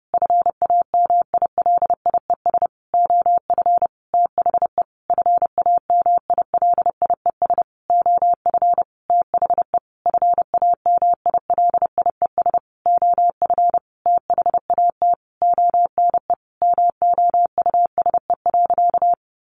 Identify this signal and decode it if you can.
Morse code, heard as a keyed tone, about 28 wpm, 700 Hz. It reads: FAMILIES OF THE FAMILIES OF THE FAMILIES OF THAT ONE MOUSE.
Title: A Sound Of Thunder 30wpm